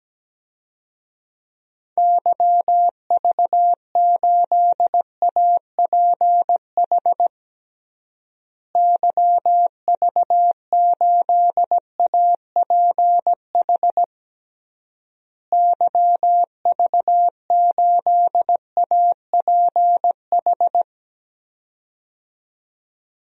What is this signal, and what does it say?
Morse code, heard as a keyed tone, about 17 wpm, 700 Hz. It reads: YV8APH YV8APH YV8APH
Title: DXCC-100 Call Signs - 6 Characters - Repeated 3x - 17wpm